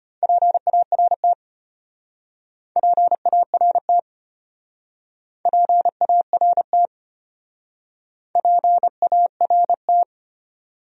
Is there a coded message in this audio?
Yes: Morse code from 0.2 to 10.0 s, heard as a keyed tone, about 32 wpm, 700 Hz. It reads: PART PART PART PART